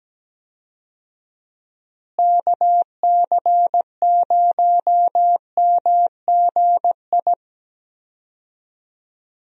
Morse code, heard as a keyed tone, about 17 wpm, 700 Hz. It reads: KC0MGI
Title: USA Call Signs - 6 Characters - Repeated 3x - 17wpm